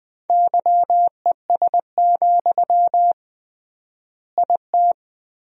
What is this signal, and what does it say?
Morse code, heard as a keyed tone, about 20 wpm, 700 Hz. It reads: YES, IT
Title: 3x Word-Spacing - Sentences from Top 500 Words - 20wpm